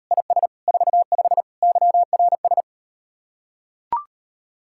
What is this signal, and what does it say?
Morse code, heard as a keyed tone, about 38 wpm, 700 Hz. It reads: IS 45 YRS